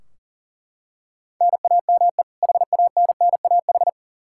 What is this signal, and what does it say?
Morse code, heard as a keyed tone, about 40 wpm, 700 Hz. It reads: NAME HANNAH
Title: Practice QSO - Exchange 1 - Beginning - 40wpm